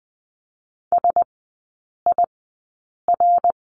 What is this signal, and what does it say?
Morse code, heard as a keyed tone, about 20 wpm, spaced as for 8 wpm, 700 Hz. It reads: SIR
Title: 093 Lesson 16e - Letter c - Three Characters 20wpm